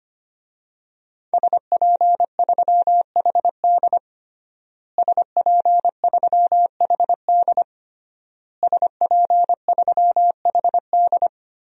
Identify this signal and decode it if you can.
Morse code, heard as a keyed tone, about 25 wpm, 700 Hz. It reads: SP3HD SP3HD SP3HD
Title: DXCC-100 Call Signs - 5 Characters - Repeated 3x - 25wpm